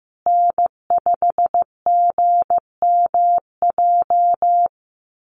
Morse code, heard as a keyed tone, about 15 wpm, 700 Hz. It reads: N5GMJ
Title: USA Call Signs Repeated 2x 15wpm